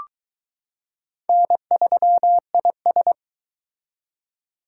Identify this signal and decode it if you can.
Morse code, heard as a keyed tone, about 21 wpm, 700 Hz. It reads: N3IS